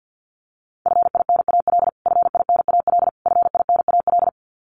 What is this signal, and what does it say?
Morse code, heard as a keyed tone, about 50 wpm, 700 Hz. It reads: FINAL FINAL FINAL
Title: Top 500 Words Repeated 50wpm